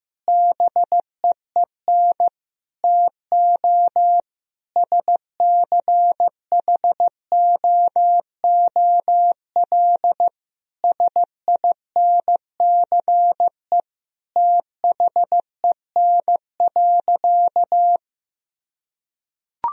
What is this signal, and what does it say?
Morse code, heard as a keyed tone, about 15 wpm, 700 Hz. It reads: BEEN TO SCHOOL SINCE THEN.
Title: Sentences from Top 1000 Words - Review Part 1 - 15wpm